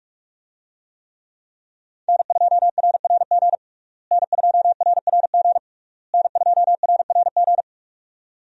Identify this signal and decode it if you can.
Morse code, heard as a keyed tone, about 46 wpm, 700 Hz. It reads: N2RRG N2RRG N2RRG
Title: USA Call Signs - 5 Characters - Repeated 3x - 45wpm